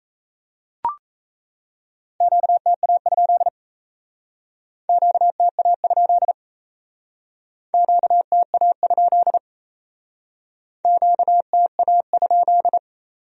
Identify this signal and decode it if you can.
Morse code, heard as a keyed tone, about 35 wpm, 700 Hz. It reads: QTA? QTA? QTA? QTA?
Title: Speed-Racing Q-Codes - 28wpm